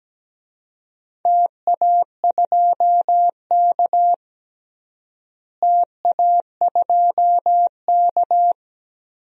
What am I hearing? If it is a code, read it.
Morse code, heard as a keyed tone, about 17 wpm, 700 Hz. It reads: TA2K TA2K
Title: DXCC-100 Call Signs - 4 Characters - Repeated 2x - 17wpm